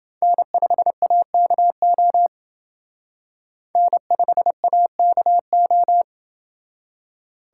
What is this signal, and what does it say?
Morse code, heard as a keyed tone, about 28 wpm, 700 Hz. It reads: N5AKO N5AKO